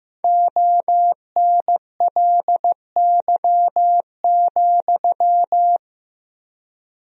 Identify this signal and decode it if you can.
Morse code, heard as a keyed tone, about 15 wpm, 700 Hz. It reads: ONLY,
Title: 3x Word-Spacing - Sentences from Top 300 Words - 15wpm